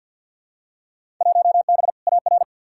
Morse code, heard as a keyed tone, about 46 wpm, 700 Hz. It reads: 1B AR
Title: Field Day Exchanges Encore - 50wpm